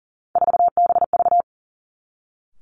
Morse code, heard as a keyed tone, about 37 wpm, 700 Hz. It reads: 4BV
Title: Three Challenging Letter-Number Combinations 40wpm